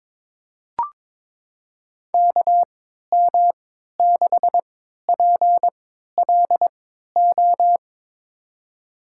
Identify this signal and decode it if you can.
Morse code, heard as a keyed tone, about 22 wpm, spaced as for 12 wpm, 700 Hz. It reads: KM6PLO
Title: USA Call Signs - 3x Character Spacing - 22wpm